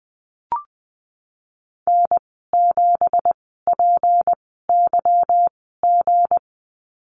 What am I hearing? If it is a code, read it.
Morse code, heard as a keyed tone, about 20 wpm, spaced as for 15 wpm, 700 Hz. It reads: N7PYG
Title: 179 Lesson 29g - Slash - Call Signs 20wpm